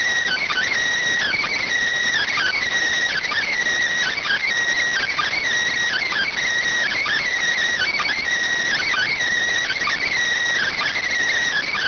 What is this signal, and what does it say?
Emergency Alert System (SAME) tones (header bursts), heard as harsh buzzing bursts. It is heard from 0.2 to 11.1 s.